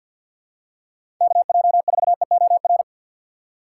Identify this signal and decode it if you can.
Morse code, heard as a keyed tone, about 51 wpm, 690 Hz. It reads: KJ4EOR
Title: USA Call Signs - 6 Characters - 3x Character Spacing - 50wpm